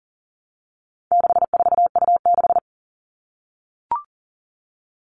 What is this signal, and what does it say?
Morse code, heard as a keyed tone, about 40 wpm, 700 Hz. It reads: 64U6
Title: All Cut Numbers - 4-Digit 40wpm